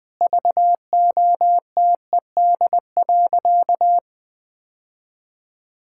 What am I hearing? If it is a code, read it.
Morse code, heard as a keyed tone, about 20 wpm, 700 Hz. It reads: VOTED.
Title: Sentences with 3-Word Phrases - Spoken Beginning - 20wpm